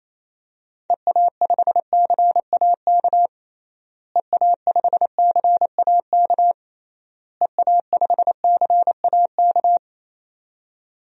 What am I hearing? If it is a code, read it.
Morse code, heard as a keyed tone, about 27 wpm, 700 Hz. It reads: EA5CAK EA5CAK EA5CAK
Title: DXCC-100 Call Signs - 6 Characters - Repeated 3x - 28wpm